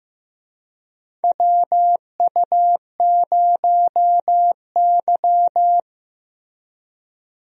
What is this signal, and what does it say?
Morse code, heard as a keyed tone, about 15 wpm, 700 Hz. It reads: WU0Y
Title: USA Call Signs - 4 Characters - Repeated 2x - 15wpm